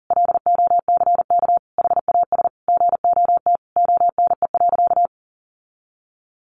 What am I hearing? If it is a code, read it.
Morse code, heard as a keyed tone, about 37 wpm, 700 Hz. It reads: LOCK HAS GOT ONE.
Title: Sentences from Top 1300 Words 40wpm